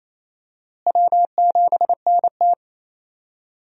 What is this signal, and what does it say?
Morse code, heard as a keyed tone, about 28 wpm, 700 Hz. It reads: W7NT
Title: USA Call Signs - 4 Characters - Repeated 3x - 28wpm